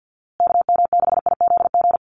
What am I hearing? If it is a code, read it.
Morse code, heard as a keyed tone, about 50 wpm, 700 Hz. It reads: KM6IZG
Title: USA Call Signs Repeated 3x 50wpm